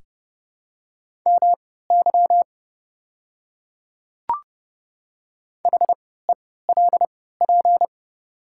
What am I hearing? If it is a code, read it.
Morse code, heard as a keyed tone, about 29 wpm, spaced as for 17 wpm, 700 Hz. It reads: MY HELP
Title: 3x Character Spacing - Top 300 Words - 30wpm